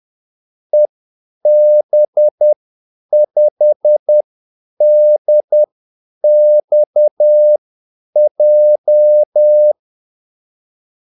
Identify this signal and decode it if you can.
Morse code, heard as a keyed tone, about 10 wpm, spaced as for 8 wpm, 600 Hz. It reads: EB5DXJ
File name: Callsigns-10-8wpm